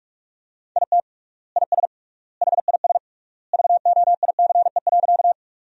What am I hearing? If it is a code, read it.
Morse code, heard as a keyed tone, about 46 wpm, 700 Hz. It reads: IT IS HIS VOICE.
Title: 3x Word-Spacing - Sentences from Top 500 Words - 45wpm